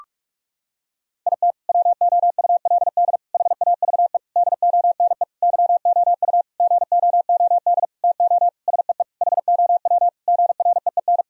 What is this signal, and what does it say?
Morse code, heard as a keyed tone, about 46 wpm, 700 Hz. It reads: IT WOULD HAVE DONE YOU GOOD TO SEE HOW GREEN
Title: Sentences from Top 1000 Words - Review Part 2 - 45wpm